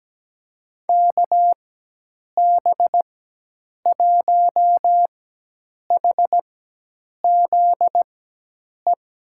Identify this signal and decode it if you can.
Morse code, heard as a keyed tone, about 17 wpm, spaced as for 8 wpm, 700 Hz. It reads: KB1HZE
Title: USA Call Signs - 6 Characters - 4x Character Spacing - 17wpm